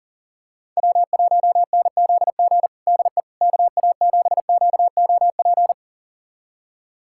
Morse code, heard as a keyed tone, about 40 wpm, 700 Hz. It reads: W1NZG DE KA7QOP
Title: USA Callsign DE Callsign 40wpm